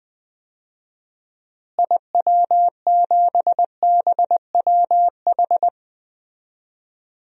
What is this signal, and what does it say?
Morse code, heard as a keyed tone, about 20 wpm, 700 Hz. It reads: IW7BWH
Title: DXCC-100 Call Signs - 6 Characters - Repeated 2x - 20wpm